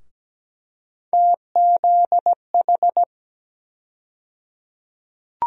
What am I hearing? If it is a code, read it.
Morse code, heard as a keyed tone, about 17 wpm, 700 Hz. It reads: TZH
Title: USA Call Sign Postfixes - 3 Letters 17wpm